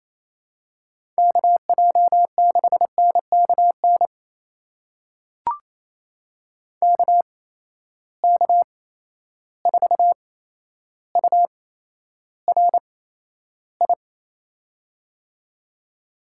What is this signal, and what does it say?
Morse code, heard as a keyed tone, about 28 wpm, spaced as for 12 wpm, 700 Hz. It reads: KJ6NKN KK4URI
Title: USA Call Signs - 6 Characters - 8x Character Spacing - 28wpm